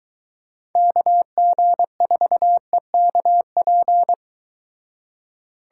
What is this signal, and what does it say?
Morse code, heard as a keyed tone, about 23 wpm, 700 Hz. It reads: KG4EKP